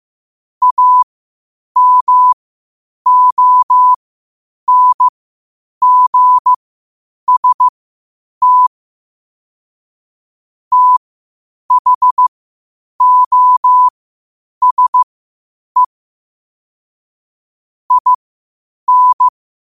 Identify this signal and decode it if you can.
Morse code, heard as a keyed tone, about 15 wpm, spaced as for 8 wpm, 1000 Hz. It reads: AMONGST THOSE IN
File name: qotdfiles / 8wpm